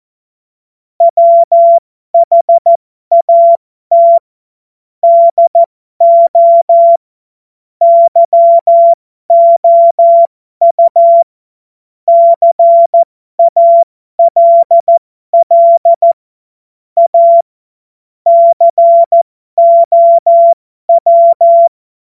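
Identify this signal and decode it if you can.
Morse code, heard as a keyed tone, about 14 wpm, spaced as for 12 wpm, 680 Hz. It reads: WHAT DO YOU CALL A COW
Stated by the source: Amateur radio CW practice text made of jokes.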